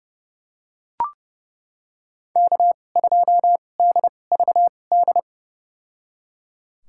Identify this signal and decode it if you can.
Morse code, heard as a keyed tone, about 30 wpm, spaced as for 22 wpm, 700 Hz. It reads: K2DVD
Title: USA Call Signs - 5 Characters - 2x Character Spacing - 30wpm